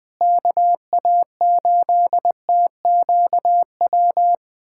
Morse code, heard as a keyed tone, about 20 wpm, 700 Hz. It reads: KA8TQW
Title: USA Callsign DE Callsign 20wpm